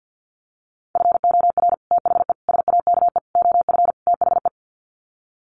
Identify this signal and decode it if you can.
Morse code, heard as a keyed tone, about 50 wpm, 700 Hz. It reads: FOR THE SAKE OF THE
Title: Sets of 5 Words Encore 50wpm